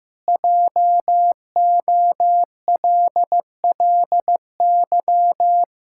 Morse code, heard as a keyed tone, about 15 wpm, 700 Hz. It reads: JOLLY
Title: Pangram Challenge v2 - 15wpm